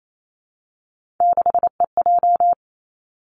Morse code, heard as a keyed tone, about 28 wpm, 700 Hz. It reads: 6EJ
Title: Three Challenging Letter-Number Combinations 28wpm